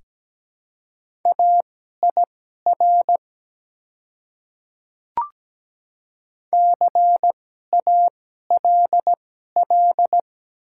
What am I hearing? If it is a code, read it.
Morse code, heard as a keyed tone, about 17 wpm, spaced as for 12 wpm, 690 Hz. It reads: AIR CALL
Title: 2x Character Spacing - Top 300 Words - 17wpm